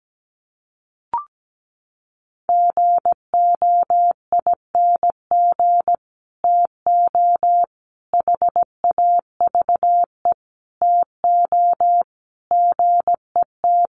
Morse code, heard as a keyed tone, about 17 wpm, 700 Hz. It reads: GOING TO HAVE TO GET
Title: Sets of 5 Words Encore2 17wpm